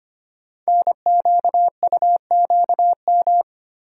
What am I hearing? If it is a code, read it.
Morse code, heard as a keyed tone, about 25 wpm, 700 Hz. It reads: NQUQM
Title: Code Groups - 5 Letters - 25wpm